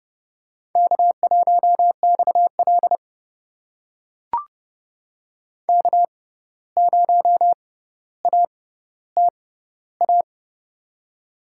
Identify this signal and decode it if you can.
Morse code, heard as a keyed tone, about 30 wpm, spaced as for 14 wpm, 700 Hz. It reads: K1XL K0ATA
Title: USA Call Signs - 6x Character Spacing - 30wpm